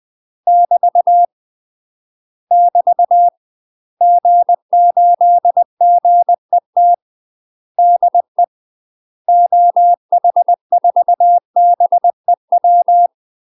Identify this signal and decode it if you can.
Morse code, heard as a keyed tone, about 20 wpm, 700 Hz. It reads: <BT> <BT> G8GET DE OH4BEW
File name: UK MASTER QSOs TRAINING / 20WPM / UK MASTER QSOs 0024